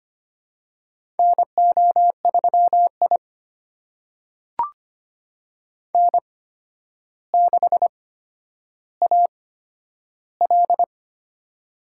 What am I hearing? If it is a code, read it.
Morse code, heard as a keyed tone, about 25 wpm, spaced as for 11 wpm, 700 Hz. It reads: NO3I N6AL
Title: USA Call Signs - 4 Characters - 8x Character Spacing - 25wpm